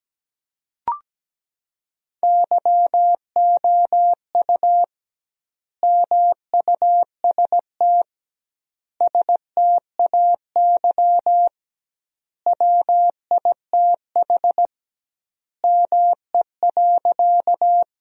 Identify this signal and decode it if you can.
Morse code, heard as a keyed tone, about 17 wpm, 700 Hz. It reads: YOU MUST STAY WITH ME.
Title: 2x Word-Spacing - Sentences from Top 500 Words - 17wpm